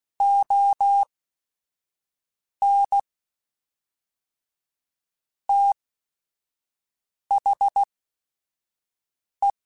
Morse code, heard as a keyed tone, about 16 wpm, spaced as for 5 wpm, 800 Hz. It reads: ON THE
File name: Chuckles / Chuckles-mp3s / HumanErr @5